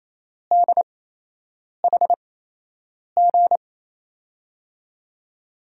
Morse code, heard as a keyed tone, about 28 wpm, spaced as for 8 wpm, 700 Hz. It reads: DHG
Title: USA Call Signs - 8x Character Spacing - 28wpm